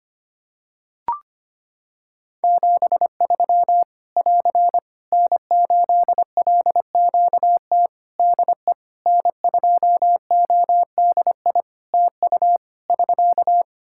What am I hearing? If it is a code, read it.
Morse code, heard as a keyed tone, about 25 wpm, 700 Hz. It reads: 73 <AR> N8LQT DE N2ODI TU <SK>
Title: K1USN SST QSO Practice 25wpm